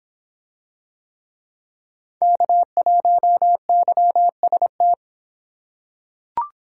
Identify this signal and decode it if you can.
Morse code, heard as a keyed tone, about 23 wpm, 700 Hz. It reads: K1YST